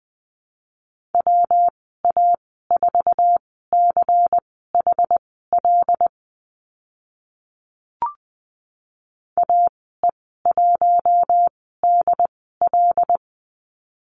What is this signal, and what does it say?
Morse code, heard as a keyed tone, about 20 wpm, spaced as for 15 wpm, 700 Hz. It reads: WA4CHL AE1DL
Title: 128 Lesson 21g - Number 6 - Call Signs 20wpm